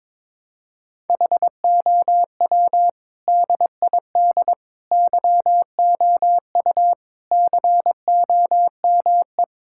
Morse code, heard as a keyed tone, about 22 wpm, 700 Hz. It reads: HOW DID YOU COME
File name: practice / 22wpm / Sets of 8 Words - 22wpm